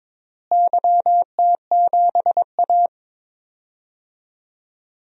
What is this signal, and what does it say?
Morse code, heard as a keyed tone, about 21 wpm, 700 Hz. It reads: YT7A
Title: DXCC-100 Call Signs - 4 Characters - Repeated 3x - 22wpm